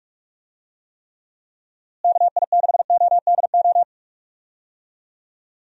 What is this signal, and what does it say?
Morse code, heard as a keyed tone, about 46 wpm, 700 Hz. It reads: KI6ODO
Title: USA Call Signs - 6 Characters - Repeated 3x - 45wpm